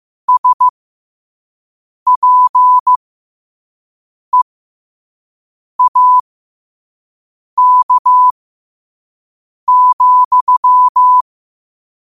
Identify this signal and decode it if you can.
Morse code, heard as a keyed tone, about 15 wpm, spaced as for 5 wpm, 1000 Hz. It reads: SPEAK,
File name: qotdfiles / 5wpm